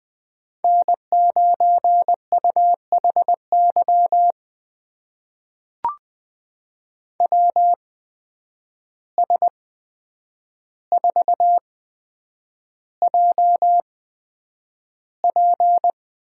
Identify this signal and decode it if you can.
Morse code, heard as a keyed tone, about 20 wpm, spaced as for 9 wpm, 700 Hz. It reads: N9UHY WS4JP
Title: USA Call Signs - 5 Characters - 8x Character Spacing - 20wpm